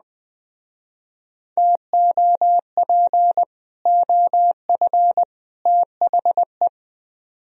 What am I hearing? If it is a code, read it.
Morse code, heard as a keyed tone, about 20 wpm, 700 Hz. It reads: TOP OF THE